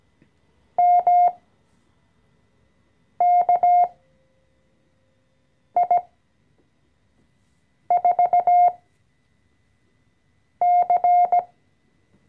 Morse code, heard as a keyed tone, about 17 wpm, spaced as for 4 wpm, 710 Hz. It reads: MKI4C
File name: CW-Online-Kurs / Anfänger / Lektion 014